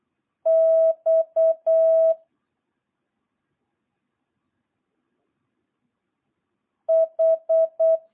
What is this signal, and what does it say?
Morse code, heard as a keyed tone, about 8 wpm, 650 Hz. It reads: X H